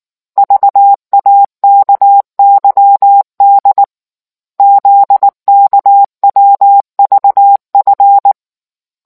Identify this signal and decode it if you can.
Morse code, heard as a keyed tone, about 19 wpm, 800 Hz. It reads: VAKYD ZKWVF